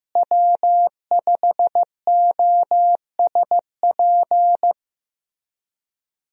Morse code, heard as a keyed tone, about 14 wpm, 700 Hz. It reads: W5OSP
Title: USA Call Signs - 5 Characters - Repeated 3x - 15wpm